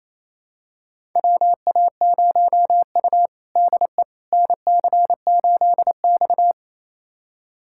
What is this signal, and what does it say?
Morse code, heard as a keyed tone, about 28 wpm, 700 Hz. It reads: WA0U DE NC8X
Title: USA Callsign DE Callsign - 4 Characters - 28wpm